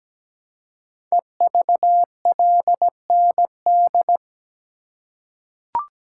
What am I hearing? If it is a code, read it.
Morse code, heard as a keyed tone, about 17 wpm, 700 Hz. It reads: EVLND
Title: Code Groups - 5 Letter:Numbers - 17wpm